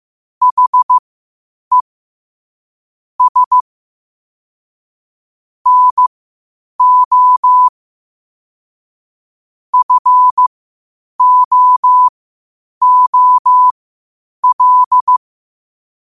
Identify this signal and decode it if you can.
Morse code, heard as a keyed tone, about 15 wpm, spaced as for 8 wpm, 1000 Hz. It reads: HE S NO FOOL